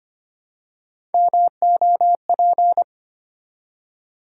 Morse code, heard as a keyed tone, about 25 wpm, 700 Hz. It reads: MOP
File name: practice / 25wpm / Easy Words with Hints - 25wpm